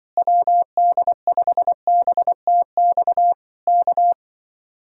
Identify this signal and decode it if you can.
Morse code, heard as a keyed tone, about 25 wpm, 700 Hz. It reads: WD5BTX K